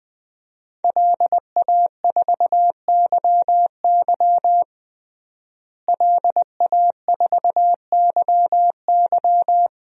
Morse code, heard as a keyed tone, about 21 wpm, 700 Hz. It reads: LA4YY LA4YY
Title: DXCC-100 Call Signs - 5 Characters - Repeated 2x - 20wpm